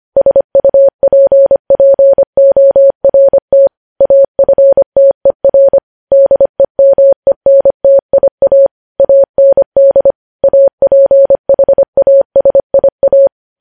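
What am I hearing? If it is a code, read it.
Morse code, heard as a keyed tone, about 25 wpm, 550 Hz. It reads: SUPPORT AFTER DEMENTIA AND APHASIA
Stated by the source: Morse code practice of news headlines at 25 wpm, opening with vvv.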